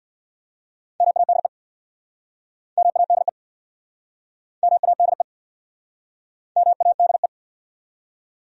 Morse code, heard as a keyed tone, about 69 wpm, 700 Hz. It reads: MADE MADE MADE MADE